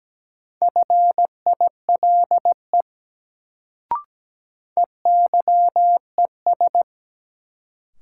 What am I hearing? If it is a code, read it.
Morse code, heard as a keyed tone, about 17 wpm, 700 Hz. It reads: FILE EYES
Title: Sentences from Top 1000 Words - Review Part 4 - New Word Drill - 17wpm